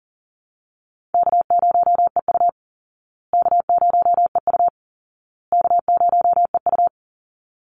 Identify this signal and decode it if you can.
Morse code, heard as a keyed tone, about 40 wpm, 700 Hz. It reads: K0EU K0EU K0EU
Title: Call Signs Repeated 2x - 40wpm